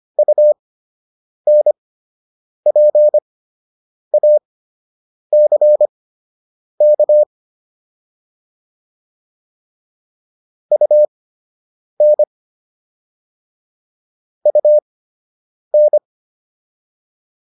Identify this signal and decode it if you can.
Morse code, heard as a keyed tone, about 25 wpm, spaced as for 8 wpm, 600 Hz. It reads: UNPACK UN UN